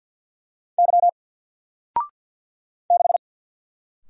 Morse code, heard as a keyed tone, about 50 wpm, 700 Hz. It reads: Y 6
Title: Single Character (Letter:Number:Punctuation:Prosign) 50wpm